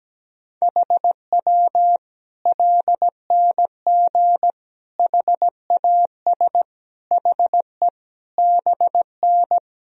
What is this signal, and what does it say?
Morse code, heard as a keyed tone, about 17 wpm, 700 Hz. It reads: HW LNG HAS HE BN
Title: Sentences from Top 100 Words and Abbreviations 17wpm